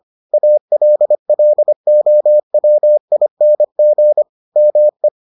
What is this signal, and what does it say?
Morse code, heard as a keyed tone, about 25 wpm, 600 Hz. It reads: ALLOWING ME